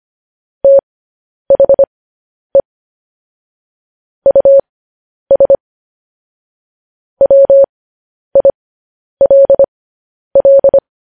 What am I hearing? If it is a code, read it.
Morse code, heard as a keyed tone, about 25 wpm, spaced as for 10 wpm, 550 Hz. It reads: THE US WILL
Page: news-headlines-in-morse-code-at-10-wpm